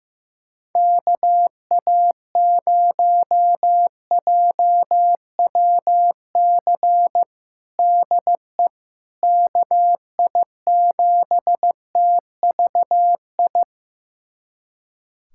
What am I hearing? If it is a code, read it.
Morse code, heard as a keyed tone, about 15 wpm, 700 Hz. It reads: KA0JWC DE KI7TVI
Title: USA Callsign DE Callsign - 6 Characters - 15wpm